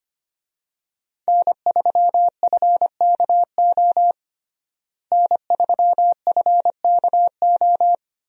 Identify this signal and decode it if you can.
Morse code, heard as a keyed tone, about 25 wpm, 700 Hz. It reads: N3FKO N3FKO
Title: USA Call Signs - 5 Characters - Repeated 2x - 25wpm